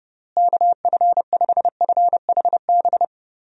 Morse code, heard as a keyed tone, about 31 wpm, 700 Hz. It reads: KF5FHB
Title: USA Call Signs - 6 Characters - Repeated 3x - 30wpm